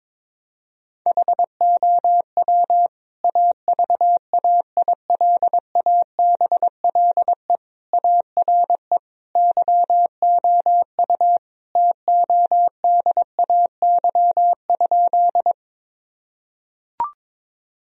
Morse code, heard as a keyed tone, about 22 wpm, 700 Hz. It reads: HOW AVAILABLE ARE YOU TODAY?
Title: Sentences from Top 1000 Words - Review Part 3 - 22wpm